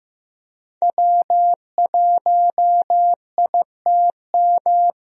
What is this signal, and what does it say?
Morse code, heard as a keyed tone, about 15 wpm, 700 Hz. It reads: W1ITM
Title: USA Call Signs - 5 Characters - Repeated 2x - 15wpm